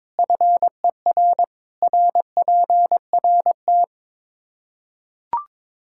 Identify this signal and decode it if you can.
Morse code, heard as a keyed tone, about 21 wpm, 700 Hz. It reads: FER RPRT
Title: Practice QSO - Exchange 2 - Info - 22wpm